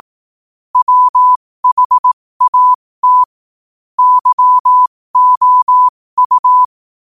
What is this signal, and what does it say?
Morse code, heard as a keyed tone, about 18 wpm, spaced as for 15 wpm, 1000 Hz. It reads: WHAT YOU
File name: qotdfiles / 15wpm